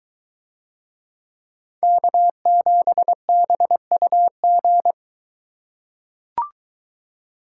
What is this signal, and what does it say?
Morse code, heard as a keyed tone, about 22 wpm, 700 Hz. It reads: K7BUG